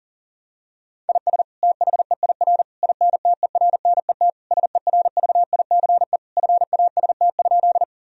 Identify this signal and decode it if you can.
Morse code, heard as a keyed tone, about 37 wpm, 700 Hz. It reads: IS THEIR INTERNET SERVICE FAST?
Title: Sentences from Top 1000 Words - Review Part 4 - 40wpm